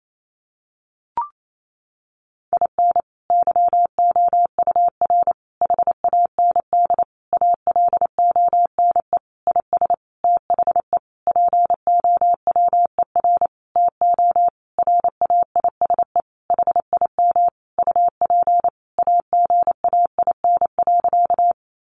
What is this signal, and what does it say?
Morse code, heard as a keyed tone, about 28 wpm, 700 Hz. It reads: IN YOUR HAND ALONE IS THE POWER TO RAISE HIM UP AGAIN.
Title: Sentences from Top 1400 Words 28wpm